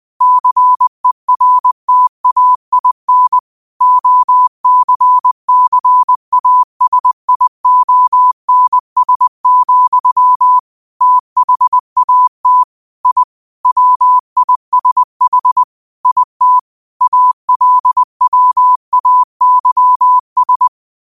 Morse code, heard as a keyed tone, about 20 wpm, 1000 Hz. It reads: CERTAIN OCCASIONS, THAT I WISH IT ALWAYS
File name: qotdfiles / 20wpm